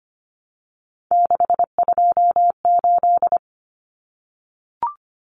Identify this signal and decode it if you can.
Morse code, heard as a keyed tone, about 25 wpm, 700 Hz. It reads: 628
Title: Common Cut Numbers - 3-Digit 25wpm